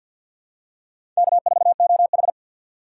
Morse code, heard as a keyed tone, about 53 wpm, 700 Hz. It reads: K4OH
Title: USA Call Signs - 3x Character Spacing - 50wpm